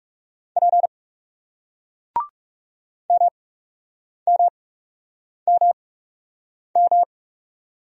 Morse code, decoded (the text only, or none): P M M M M